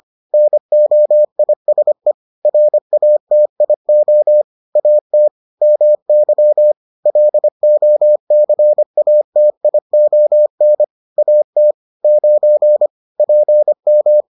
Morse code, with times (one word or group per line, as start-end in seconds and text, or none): 0.3-2.1 NOISE
2.5-4.4 RATIO
4.8-5.3 AT
5.6-6.7 MY
7.1-10.8 LOCATION
11.2-11.7 AT
12.0-12.9 9
13.2-14.2 PM